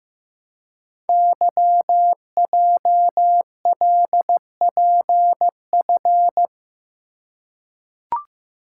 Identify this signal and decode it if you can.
Morse code, heard as a keyed tone, about 16 wpm, 700 Hz. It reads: YJLPF
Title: Code Groups - 5 Letters - 15wpm